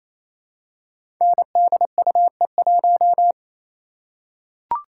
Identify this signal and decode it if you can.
Morse code, heard as a keyed tone, about 29 wpm, 700 Hz. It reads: NDUE1
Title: Code Groups - 5 Letter:Numbers - 28wpm